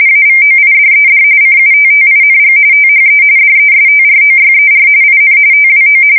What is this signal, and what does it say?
RTTY, about 45.45 baud. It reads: AVOIDED CUTTING THE DESTROYER IN TWO
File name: fsk170 / USS-New-Jersey